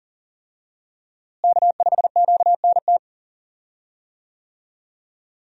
Morse code, heard as a keyed tone, about 40 wpm, 700 Hz. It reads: K5QNT
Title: USA Call Signs - 5 Characters - Repeated 2x - 40wpm